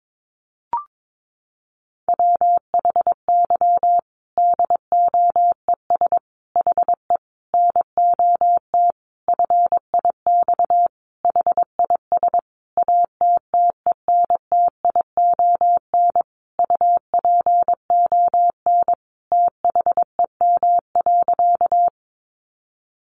Morse code, heard as a keyed tone, about 22 wpm, 700 Hz. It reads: WHY DOES HE NOT FIX HIS ATTENTION UPON THEM.
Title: Sentences from Top 1400 Words 22wpm